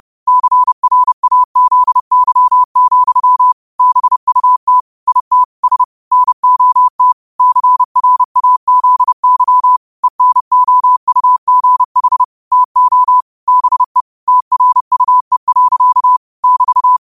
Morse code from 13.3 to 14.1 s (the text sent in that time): BE